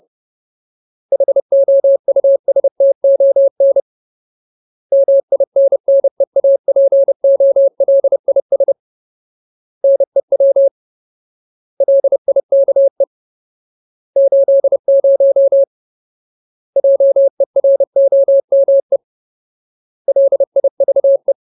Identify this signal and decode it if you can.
Morse code, heard as a keyed tone, about 30 wpm, 550 Hz. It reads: HOUSTON MINNEAPOLIS NEW LIKE 80 JEROME LIVE
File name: Ultimate Word List / 30WPM Ultimate